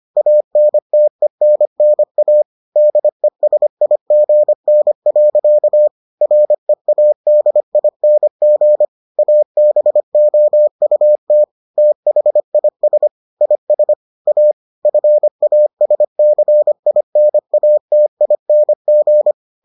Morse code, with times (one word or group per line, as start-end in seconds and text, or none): 0.2-2.4 ANTENNA
2.8-5.9 DESIGN.
6.2-8.8 READING
9.2-11.4 ABOUT
11.8-13.1 THIS
13.4-13.9 IS
14.3-14.5 A
14.9-19.3 FASCINATING